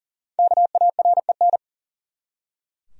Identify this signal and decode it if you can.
Morse code, heard as a keyed tone, about 40 wpm, 700 Hz. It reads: KAREN
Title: USA Common Names - Max 5 Letters - 40wpm